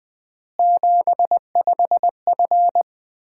Morse code, heard as a keyed tone, about 20 wpm, 700 Hz. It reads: 75F
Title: Common QSO Phrases - 4x Word-Spacing - 20wpm